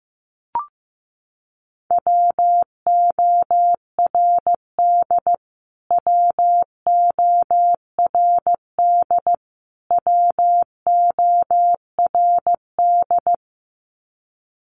Morse code, heard as a keyed tone, about 15 wpm, 700 Hz. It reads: WORD WORD WORD
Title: Top 500 Words Repeated 15wpm